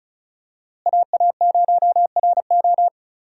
Morse code, heard as a keyed tone, about 36 wpm, 700 Hz. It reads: AA0RO
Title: USA Call Signs - 5 Characters - Repeated 3x - 35wpm